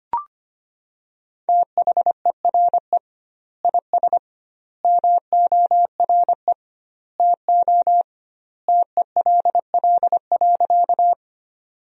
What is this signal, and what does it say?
Morse code, heard as a keyed tone, about 25 wpm, 700 Hz. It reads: THERE IS MORE TO TELL.
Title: 2x Word-Spacing - Sentences from Top 200 Words - 25wpm